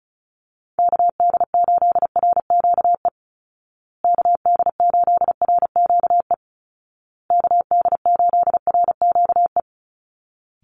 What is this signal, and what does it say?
Morse code, heard as a keyed tone, about 34 wpm, 700 Hz. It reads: KD8RQE KD8RQE KD8RQE
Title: Call Signs Repeated 2x - 35wpm